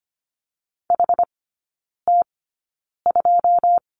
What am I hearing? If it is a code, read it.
Morse code, heard as a keyed tone, about 25 wpm, spaced as for 9 wpm, 700 Hz. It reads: HT2
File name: Lesson 17e - Letter u - Three Characters 25wpm